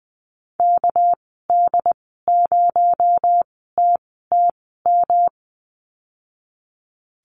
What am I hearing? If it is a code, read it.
Morse code, heard as a keyed tone, about 20 wpm, spaced as for 14 wpm, 700 Hz. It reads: KD0TTM
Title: 231 Lesson 37g - Letter x - Call Signs 20wpm